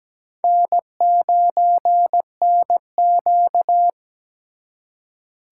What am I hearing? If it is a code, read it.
Morse code, heard as a keyed tone, about 20 wpm, 700 Hz. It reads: N9NQ
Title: Speed Racing - USA 1x2 Call Signs 17wpm